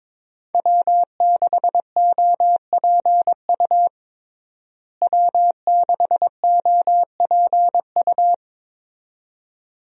Morse code, heard as a keyed tone, about 22 wpm, 700 Hz. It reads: W6OPU W6OPU
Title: USA Call Signs - 5 Characters - Repeated 2x - 22wpm